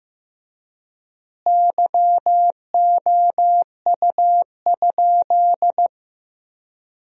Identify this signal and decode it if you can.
Morse code, heard as a keyed tone, about 15 wpm, 700 Hz. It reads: YOU?
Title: 4x Word-Spacing - Sentences from Top 100 Words - 15wpm